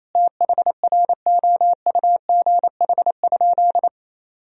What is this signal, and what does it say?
Morse code, heard as a keyed tone, about 27 wpm, 700 Hz. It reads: THROUGH?
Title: Sentences from Top 1000 Words - Review Part 2 - 28wpm